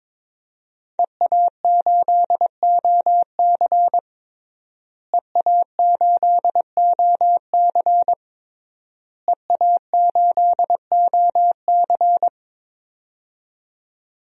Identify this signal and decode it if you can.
Morse code, heard as a keyed tone, about 22 wpm, 700 Hz. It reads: EA8OC EA8OC EA8OC
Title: DXCC-100 Call Signs - 5 Characters - Repeated 3x - 22wpm